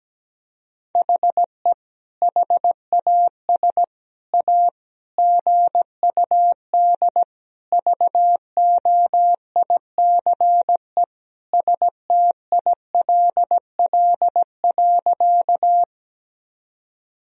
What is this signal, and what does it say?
Morse code, heard as a keyed tone, about 17 wpm, 700 Hz. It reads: HE HAS A GUD VOICE STILL.
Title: Sentences from Top 500 Words and Abbreviations 17wpm